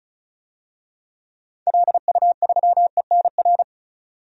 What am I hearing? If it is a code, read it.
Morse code, heard as a keyed tone, about 34 wpm, 700 Hz. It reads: LU3ENR
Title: DXCC-100 Call Signs - 6 Characters - 35wpm